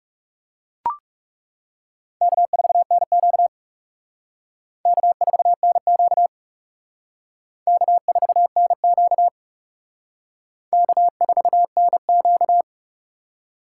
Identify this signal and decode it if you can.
Morse code, heard as a keyed tone, about 37 wpm, 700 Hz. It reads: K4NQ K4NQ K4NQ K4NQ
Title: Speed Racing - USA 1x2 Call Signs 30wpm